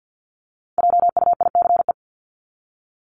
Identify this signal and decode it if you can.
Morse code, heard as a keyed tone, about 50 wpm, 700 Hz. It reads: JUICE
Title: Top 1251-1500 Words 50wpm